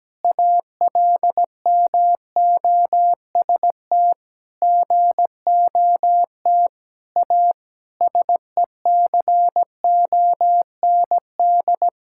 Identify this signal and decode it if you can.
Morse code, heard as a keyed tone, about 17 wpm, 700 Hz. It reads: ALMOST GOT A SECOND
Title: Sentences from Top 1000 Words - Review Part 5 - 17wpm